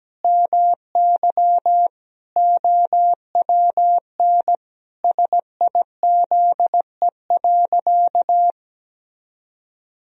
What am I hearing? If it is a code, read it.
Morse code, heard as a keyed tone, about 17 wpm, 700 Hz. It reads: MY OWN SIZE.
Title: Sentences from Top 1000 Words - Review Part 2 - 17wpm